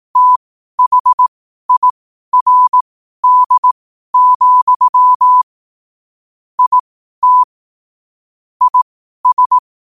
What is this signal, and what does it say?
Morse code, heard as a keyed tone, about 18 wpm, spaced as for 12 wpm, 1000 Hz. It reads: THIRD, IT IS
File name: qotdfiles / 12wpm